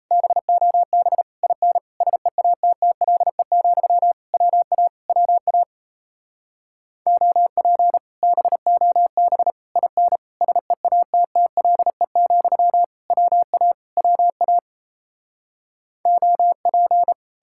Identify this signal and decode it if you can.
Morse code, heard as a keyed tone, about 37 wpm, 700 Hz. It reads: BOB IN SEATTLE, WA WA OP BOB IN SEATTLE, WA WA OP